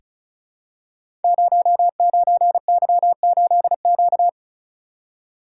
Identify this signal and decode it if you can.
Morse code, heard as a keyed tone, about 36 wpm, 700 Hz. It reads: 09Y8Q
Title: Code Groups - 5 Letter:Numbers - 35wpm